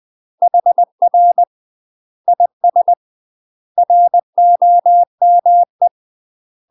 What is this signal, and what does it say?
Morse code, heard as a keyed tone, about 20 wpm, 700 Hz. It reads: HR IS ROME
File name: UK MASTER QSOs TRAINING / 20WPM / UK MASTER QSOs 0062